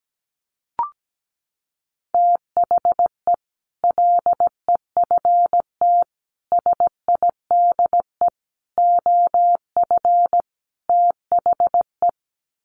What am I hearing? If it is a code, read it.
Morse code, heard as a keyed tone, about 17 wpm, 700 Hz. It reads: THE LEFT SIDE OF THE
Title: Sets of 5 Words Encore 17wpm